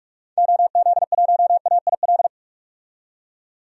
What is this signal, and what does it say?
Morse code, heard as a keyed tone, about 44 wpm, 690 Hz. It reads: OZ1AIL
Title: DXCC-100 Call Signs - 6 Characters - 45wpm